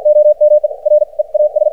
Morse code, heard as a keyed tone, about 57 wpm, spaced as for 45 wpm, 590 Hz. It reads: OME REAS
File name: archived / CRSnet